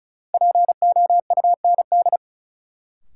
Morse code, heard as a keyed tone, about 36 wpm, 700 Hz. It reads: POUND
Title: Sentences from Top 1000 Words - Review Part 2 - New Word Drill - 35wpm